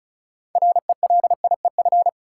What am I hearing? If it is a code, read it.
Morse code, heard as a keyed tone, about 36 wpm, 700 Hz. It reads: RELIEF
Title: Six Letter Words - 35wpm